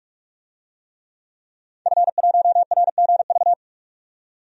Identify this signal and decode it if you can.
Morse code, heard as a keyed tone, about 46 wpm, 700 Hz. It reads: F1RGV
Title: DXCC-100 Call Signs - 5 Characters - 45wpm